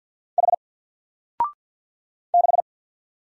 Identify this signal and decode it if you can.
Morse code, heard as a keyed tone, about 50 wpm, 700 Hz. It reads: H 6
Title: Single Letter-Number Alt 50wpm